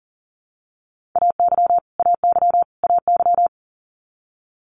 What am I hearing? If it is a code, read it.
Morse code, heard as a keyed tone, about 40 wpm, 690 Hz. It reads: AY AY AY
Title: Multi-letter Phonemes Repeated 3x 40wpm